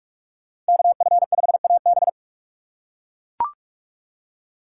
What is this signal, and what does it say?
Morse code, heard as a keyed tone, about 42 wpm, 700 Hz. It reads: KF5AB
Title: USA Call Signs - 2x Character Spacing - 45wpm